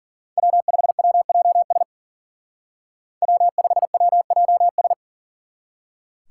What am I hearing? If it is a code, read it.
Morse code, heard as a keyed tone, about 51 wpm, 700 Hz. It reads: W5WJS W5WJS